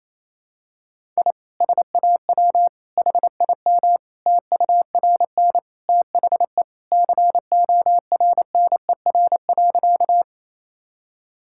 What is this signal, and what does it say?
Morse code, heard as a keyed tone, about 28 wpm, 700 Hz. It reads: I SAW HIM TURN THE CORNER.
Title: Sentences from Top 1600 Words 28wpm